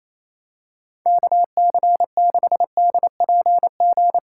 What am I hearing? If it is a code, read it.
Morse code, heard as a keyed tone, about 28 wpm, 700 Hz. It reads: KC6DPG
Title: USA Callsign DE Callsign - 6 Characters - 28wpm